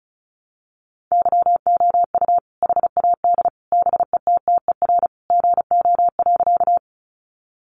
Morse code, heard as a keyed tone, about 34 wpm, 700 Hz. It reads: YOU HAD BETTER GO.
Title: Sentences from Top 400 Words 35wpm